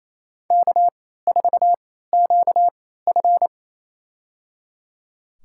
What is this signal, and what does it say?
Morse code, heard as a keyed tone, about 28 wpm, spaced as for 16 wpm, 700 Hz. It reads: K4QF
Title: USA Call Signs - 3x Character Spacing - 28wpm